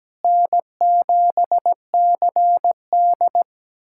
Morse code, heard as a keyed tone, about 17 wpm, 700 Hz. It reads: N7CD
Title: USA Callsign DE Callsign - 4 Characters - 17wpm